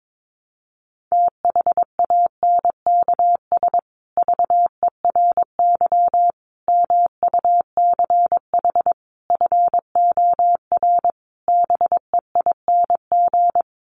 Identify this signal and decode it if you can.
Morse code, heard as a keyed tone, about 22 wpm, 700 Hz. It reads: THANKS VERY MUCH FOR BEING
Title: Sets of 5 Words Encore 22wpm